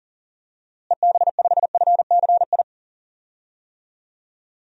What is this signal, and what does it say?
Morse code, heard as a keyed tone, about 40 wpm, 700 Hz. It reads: EB5FCI
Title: DXCC-100 Call Signs - 6 Characters - Repeated 3x - 40wpm